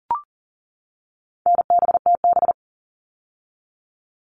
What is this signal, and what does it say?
Morse code, heard as a keyed tone, about 39 wpm, 700 Hz. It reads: NBTB
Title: All Cut Numbers - 4-Digit 40wpm